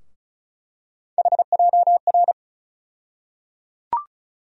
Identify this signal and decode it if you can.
Morse code, heard as a keyed tone, about 34 wpm, 700 Hz. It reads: HJR
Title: USA Call Sign Postfixes - 3 Letters 35wpm